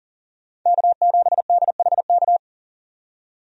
Morse code, heard as a keyed tone, about 40 wpm, 700 Hz. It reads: K7DHK
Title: USA Call Signs - 5 Characters - 40wpm